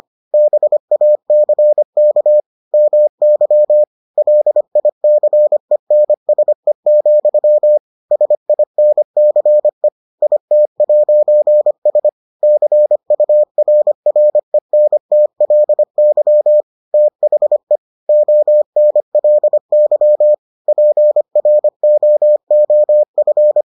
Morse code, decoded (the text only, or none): BACK MY LICENSE, SINCE IT'S CURRENTLY THE ONLY PROOF